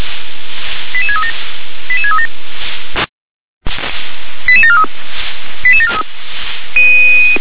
Five-tone selective call (ZVEI), heard as quick runs of five tones, tones of about 80 ms. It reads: 88637 88637 88743 88743
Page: /SDR/ZVEI/